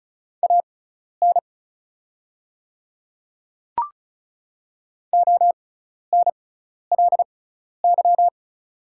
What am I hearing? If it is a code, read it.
Morse code, heard as a keyed tone, about 36 wpm, spaced as for 12 wpm, 700 Hz. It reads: AN ONLY
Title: 6x Character Spacing - Top 300 Words - 35wpm